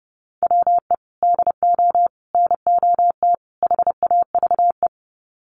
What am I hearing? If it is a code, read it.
Morse code, heard as a keyed tone, about 29 wpm, 700 Hz. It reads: WE DO NOT HAVE
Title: Sets of 4 Words 30wpm